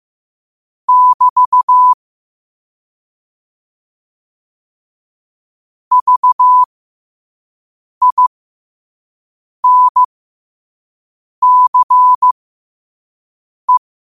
Morse code, heard as a keyed tone, about 15 wpm, spaced as for 5 wpm, 1000 Hz. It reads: <BT> VINCE
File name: qotdfiles / 5wpm